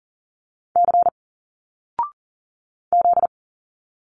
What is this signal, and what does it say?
Morse code, heard as a keyed tone, about 40 wpm, 700 Hz. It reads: C Z
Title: Single Letters 40wpm